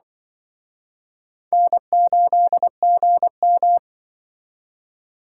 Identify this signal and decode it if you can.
Morse code, heard as a keyed tone, about 23 wpm, 700 Hz. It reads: N8GM
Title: Speed Racing - USA 1x2 Call Signs 20wpm